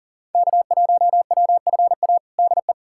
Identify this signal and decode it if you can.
Morse code, heard as a keyed tone, about 40 wpm, 700 Hz. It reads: K1WFA DE
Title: USA Callsign DE Callsign - 5 Characters - 40wpm